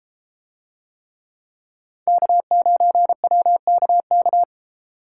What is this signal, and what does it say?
Morse code, heard as a keyed tone, about 32 wpm, 700 Hz. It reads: K9WKK